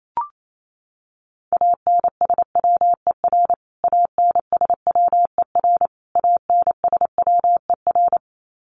Morse code, heard as a keyed tone, about 28 wpm, 700 Hz. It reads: ANSWER ANSWER ANSWER
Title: Top 300 Words Repeated 28wpm